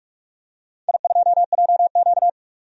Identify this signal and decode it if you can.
Morse code, heard as a keyed tone, about 42 wpm, 700 Hz. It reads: I2JQ
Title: DXCC-100 Call Signs - 4 Characters - 45wpm